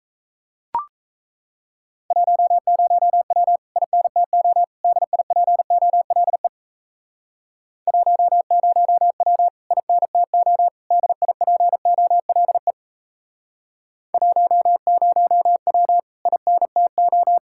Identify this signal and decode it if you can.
Morse code, heard as a keyed tone, about 39 wpm, 700 Hz. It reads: 10W INTO DIPOLE 10W INTO DIPOLE 10W INTO